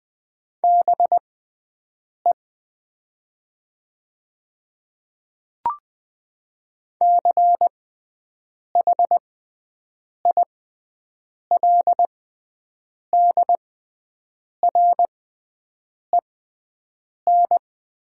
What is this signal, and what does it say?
Morse code, heard as a keyed tone, about 20 wpm, spaced as for 7 wpm, 700 Hz. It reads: BE CHILDREN
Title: 6x Character Spacing - Top 500 Words - 20wpm